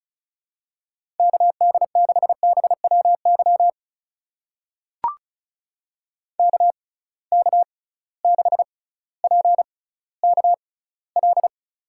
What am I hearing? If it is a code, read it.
Morse code, heard as a keyed tone, about 36 wpm, spaced as for 18 wpm, 700 Hz. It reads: KD6BWY KK6PKL
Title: USA Call Signs - 6 Characters - 6x Character Spacing - 35wpm